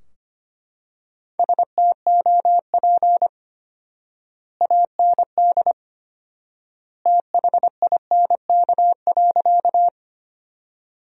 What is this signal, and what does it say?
Morse code, heard as a keyed tone, about 25 wpm, 700 Hz. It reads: STOP AND THINK.
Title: 4x Word-Spacing - Sentences from Top 300 Words - 25wpm